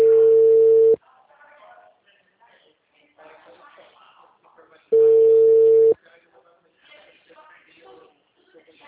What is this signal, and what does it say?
Ringback tone (China, 450 Hz), with speech.